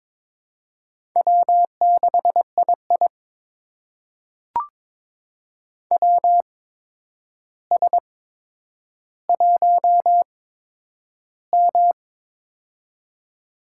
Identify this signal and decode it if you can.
Morse code, heard as a keyed tone, about 22 wpm, spaced as for 9 wpm, 700 Hz. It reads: W6II WS1M
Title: USA Call Signs - 4 Characters - 8x Character Spacing - 22wpm